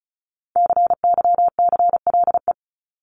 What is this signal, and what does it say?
Morse code, heard as a keyed tone, about 34 wpm, 700 Hz. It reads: CYCLE
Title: Top 751-1000 Words 35wpm